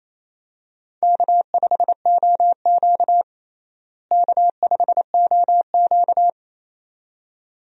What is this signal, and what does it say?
Morse code, heard as a keyed tone, about 27 wpm, 700 Hz. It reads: K5OQ K5OQ
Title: USA Call Signs - 4 Characters - Repeated 2x - 28wpm